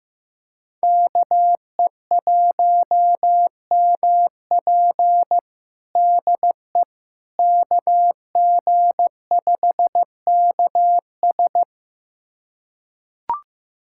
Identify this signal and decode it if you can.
Morse code, heard as a keyed tone, about 15 wpm, 700 Hz. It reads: KE1MP DE KG5KS
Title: USA Callsign DE Callsign - 5 Characters - 15wpm